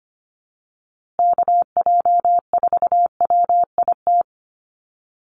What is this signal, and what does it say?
Morse code, heard as a keyed tone, about 25 wpm, 700 Hz. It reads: KJ4WIT
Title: USA Call Signs Repeated 3x 25wpm